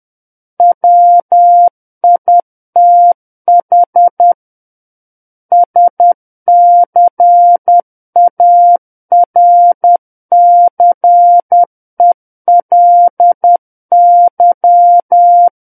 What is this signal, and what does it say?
Morse code, heard as a keyed tone, about 10 wpm, 700 Hz. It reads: WITH SCARCELY
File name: Book One / 01-Chapter 1 - 10 WPM